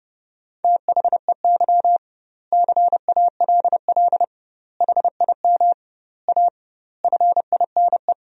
Morse code, heard as a keyed tone, about 28 wpm, 700 Hz. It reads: THEY CALL HIM A FINE
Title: 2x Word-Spacing - Sentences from Top 500 Words - 30wpm